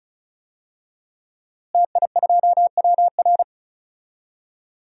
Morse code, heard as a keyed tone, about 36 wpm, 700 Hz. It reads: TI2WR
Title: DXCC-100 Call Signs - 5 Characters - Repeated 3x - 35wpm